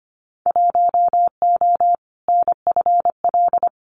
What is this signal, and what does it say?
Morse code, heard as a keyed tone, about 25 wpm, 690 Hz. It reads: 1O NFL
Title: Winter Field Day Exchanges 25wpm